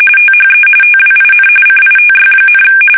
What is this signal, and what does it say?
RTTY, about 45.45 baud. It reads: THE TWIN CITY,
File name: fsk850 / teletype-RTTY